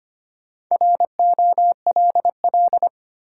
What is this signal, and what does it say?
Morse code, heard as a keyed tone, about 25 wpm, 700 Hz. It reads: ROLL
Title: Four Letter Words - 25wpm